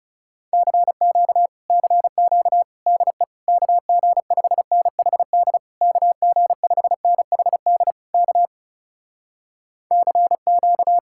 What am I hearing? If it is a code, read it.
Morse code, heard as a keyed tone, about 40 wpm, 700 Hz. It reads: CQ CQ DE KG5NHD KG5NHD K CQ